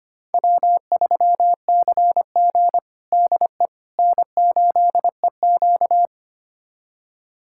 Morse code, heard as a keyed tone, about 25 wpm, 700 Hz. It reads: W3CG DE N8EQ
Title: USA Callsign DE Callsign - 4 Characters - 25wpm